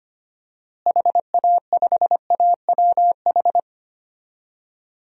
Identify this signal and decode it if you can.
Morse code, heard as a keyed tone, about 25 wpm, 700 Hz. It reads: HA5AWH
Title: DXCC-100 Call Signs - 6 Characters - Repeated 3x - 25wpm